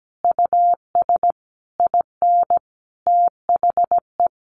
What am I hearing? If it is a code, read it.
Morse code, heard as a keyed tone, about 17 wpm, 700 Hz. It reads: US IN THE
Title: Sentences from Top 1200 Words 17wpm